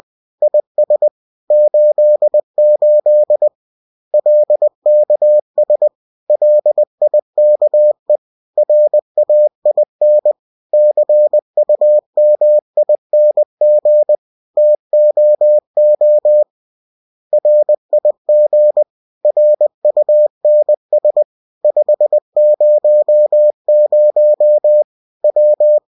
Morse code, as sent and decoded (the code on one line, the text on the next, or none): .. ... / ---.. ---.. / .-.. -.- ... / .-.. .. -.- . / .-. .- .. -. / -.-. ..- -- .. -. --. / - --- --- / .-. .. --. / .-. ..- -. ... / ..... ----- ----- / .--
IS 88 LKS LIKE RAIN CUMING TOO RIG RUNS 500 W